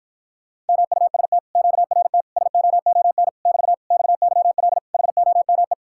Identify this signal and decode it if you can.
Morse code, heard as a keyed tone, about 53 wpm, 700 Hz. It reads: MUST QRT SOON <BT> XYL HOME